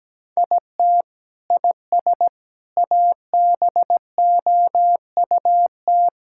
Morse code, heard as a keyed tone, about 17 wpm, 700 Hz. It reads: IT IS ABOUT
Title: Sentences from Top 1000 Words - Review Part 6 - 17wpm